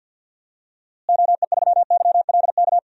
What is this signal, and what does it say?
Morse code, heard as a keyed tone, about 53 wpm, 700 Hz. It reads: OE3YLK
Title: DXCC-100 Call Signs - 6 Characters - Repeated 3x - 50wpm